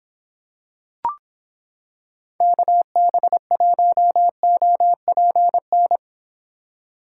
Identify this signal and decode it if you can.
Morse code, heard as a keyed tone, about 26 wpm, 700 Hz. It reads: KB1OPN